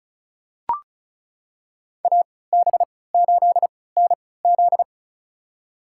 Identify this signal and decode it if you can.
Morse code, heard as a keyed tone, about 36 wpm, spaced as for 20 wpm, 700 Hz. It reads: AB8NZ
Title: USA Call Signs - 5 Characters - 3x Character Spacing - 35wpm